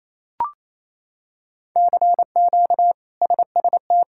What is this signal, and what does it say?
Morse code, heard as a keyed tone, about 28 wpm, 700 Hz. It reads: CQ SST
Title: K1USN SST QSO Practice - 28wpm